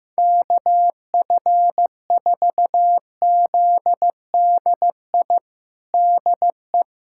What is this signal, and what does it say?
Morse code, heard as a keyed tone, about 15 wpm, 700 Hz. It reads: KF4ZDI DE
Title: USA Callsign DE Callsign - 6 Characters - 15wpm